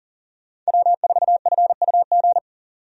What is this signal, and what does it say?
Morse code, heard as a keyed tone, about 40 wpm, 700 Hz. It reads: W4FUG
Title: USA Call Signs - 5 Characters - 40wpm